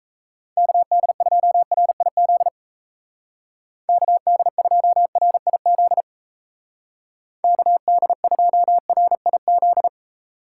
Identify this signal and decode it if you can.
Morse code, heard as a keyed tone, about 34 wpm, 700 Hz. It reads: KD2RIZ KD2RIZ KD2RIZ
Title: Speed Racing - USA 2x3 Call Signs 28wpm